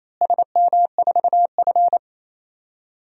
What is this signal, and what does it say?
Morse code, heard as a keyed tone, about 28 wpm, 700 Hz. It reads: SM4F
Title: DXCC-100 Call Signs - 4 Characters - Repeated 2x - 28wpm